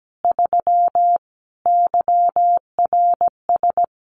Morse code, heard as a keyed tone, about 17 wpm, 700 Hz. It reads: 3 YRS
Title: Common QSO Phrases Encore 17wpm